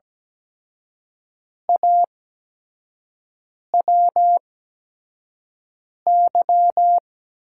Morse code, heard as a keyed tone, about 17 wpm, spaced as for 5 wpm, 700 Hz. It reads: AWY
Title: USA Call Signs - 6 Characters - 8x Character Spacing - 17wpm